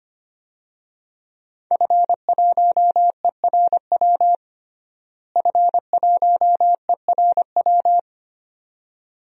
Morse code, heard as a keyed tone, about 25 wpm, 700 Hz. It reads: F1ERW F1ERW
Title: DXCC-100 Call Signs - 5 Characters - Repeated 2x - 25wpm